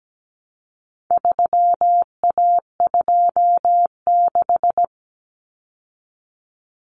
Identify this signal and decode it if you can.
Morse code, heard as a keyed tone, about 17 wpm, 700 Hz. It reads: 3A26
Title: Common Cut Numbers - 4-Digit 17wpm